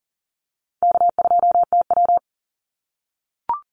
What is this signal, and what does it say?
Morse code, heard as a keyed tone, about 40 wpm, 700 Hz. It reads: K2TW
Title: Call Signs - 40wpm